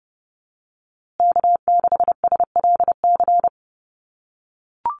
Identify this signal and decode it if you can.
Morse code, heard as a keyed tone, about 30 wpm, 700 Hz. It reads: K6SLC
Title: USA Call Signs Repeated 2x 30wpm